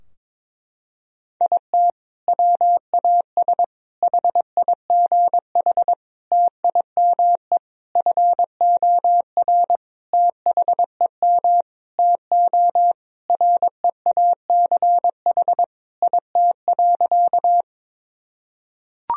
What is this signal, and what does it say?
Morse code, heard as a keyed tone, about 22 wpm, 700 Hz. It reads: IT WAS HIGH TIME FOR THEM TO REACH IT.
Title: Sentences from Top 1000 Words - Review Part 2 - 22wpm